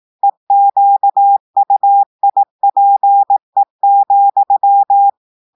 Morse code, heard as a keyed tone, about 18 wpm, 800 Hz. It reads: EQUIPE,
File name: audios / portugais / portugais texte en clair 2 vitesse 18 mots minutes